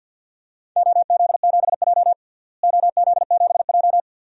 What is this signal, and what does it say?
Morse code, heard as a keyed tone, about 52 wpm, 700 Hz. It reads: OZ7J OZ7J
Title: DXCC-100 Call Signs - 4 Characters - Repeated 2x - 50wpm